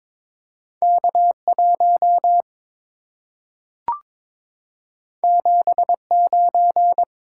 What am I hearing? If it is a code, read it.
Morse code, heard as a keyed tone, about 22 wpm, 690 Hz. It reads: K1 79
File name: practice / 22wpm / Two Letter-Number Combinations - 22wpm